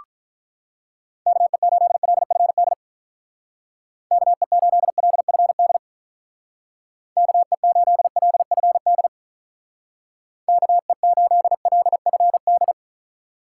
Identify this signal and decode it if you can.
Morse code, heard as a keyed tone, about 43 wpm, 700 Hz. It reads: KE8LFD KE8LFD KE8LFD KE8LFD
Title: Speed Racing - USA Call Signs 35wpm